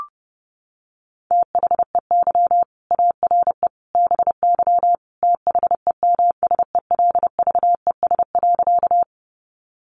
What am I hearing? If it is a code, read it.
Morse code, heard as a keyed tone, about 30 wpm, 700 Hz. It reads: THEY ARE BY THEMSELVES.
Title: Sentences from Top 800 Words 30wpm